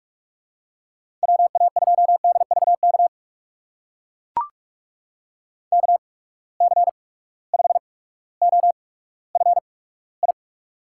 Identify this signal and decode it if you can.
Morse code, heard as a keyed tone, about 43 wpm, spaced as for 19 wpm, 700 Hz. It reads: WA2DVK KC5OFI
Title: USA Call Signs - 6 Characters - 8x Character Spacing - 45wpm